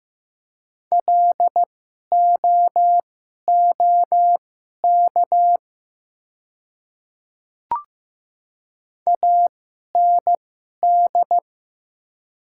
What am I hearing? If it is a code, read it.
Morse code, heard as a keyed tone, about 15 wpm, spaced as for 11 wpm, 700 Hz. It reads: LOOK AND
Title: 2x Character Spacing - Top 100 Words - 15wpm